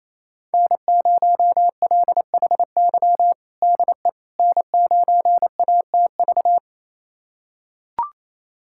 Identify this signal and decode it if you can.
Morse code, heard as a keyed tone, about 28 wpm, 700 Hz. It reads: N0LHY DE N9ATV
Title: USA Callsign DE Callsign - 5 Characters - 28wpm